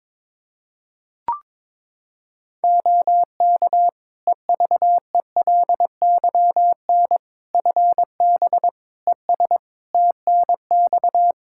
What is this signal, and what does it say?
Morse code, heard as a keyed tone, about 22 wpm, 700 Hz. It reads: OK EVELYN FB ES TNX
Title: Practice QSO - Exchange 2 - Info - 22wpm